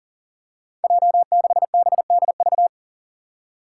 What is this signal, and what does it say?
Morse code, heard as a keyed tone, about 40 wpm, 700 Hz. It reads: J6BDV
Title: Five Challenging Letter-Number Combinations 40wpm